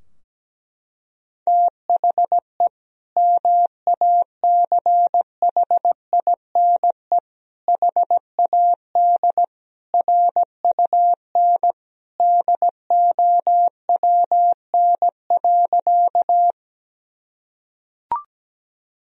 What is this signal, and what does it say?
Morse code, heard as a keyed tone, about 17 wpm, 700 Hz. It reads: THE MACHINE HAD RUN DOWN.
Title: Sentences from Top 1000 Words - Review Part 2 - 17wpm